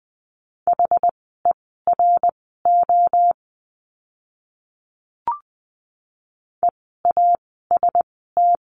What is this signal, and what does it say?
Morse code, heard as a keyed tone, about 20 wpm, spaced as for 14 wpm, 700 Hz. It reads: HERO EAST
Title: 068 Lesson 12f - Letter d - Words 20wpm